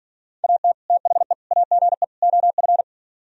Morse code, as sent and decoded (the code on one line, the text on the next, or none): .- - / - .... / .- --. / --- ..-.
AT THE AGE OF